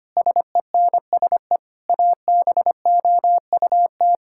Morse code, heard as a keyed tone, about 25 wpm, 700 Hz. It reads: SENSE ABOUT
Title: Sentences from Top 1000 Words - Review Part 5 - 25wpm